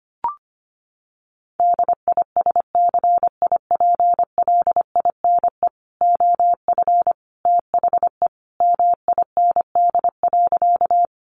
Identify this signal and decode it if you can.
Morse code, heard as a keyed tone, about 25 wpm, 700 Hz. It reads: DISCIPLINE OF THE MIND.
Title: Sentences from Top 900 Words 25wpm